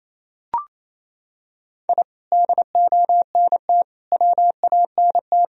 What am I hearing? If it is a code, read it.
Morse code, heard as a keyed tone, about 28 wpm, 700 Hz. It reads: I DONT WANT
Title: Sentences from Top 1000 Words - Review Part 1 - 28wpm